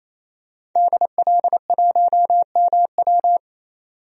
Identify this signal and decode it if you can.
Morse code, heard as a keyed tone, about 28 wpm, 700 Hz. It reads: DL1MW
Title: DXCC-100 Call Signs - 5 Characters - 28wpm